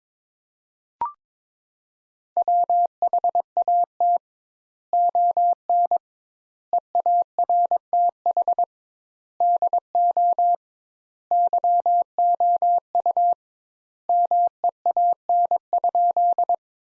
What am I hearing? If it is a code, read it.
Morse code, heard as a keyed tone, about 22 wpm, 700 Hz. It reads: WHAT ON EARTH DO YOU MEAN?
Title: Sentences Top 200 Words - 22wpm